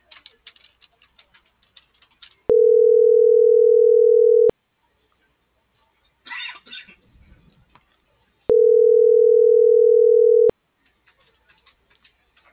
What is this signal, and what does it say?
Ringback tone (North America), 440 + 480 Hz.